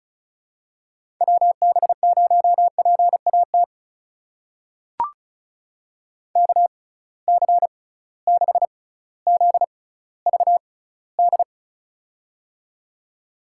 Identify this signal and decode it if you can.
Morse code, heard as a keyed tone, about 36 wpm, spaced as for 18 wpm, 700 Hz. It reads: WB0PAT KC6ZVD
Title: USA Call Signs - 6 Characters - 6x Character Spacing - 35wpm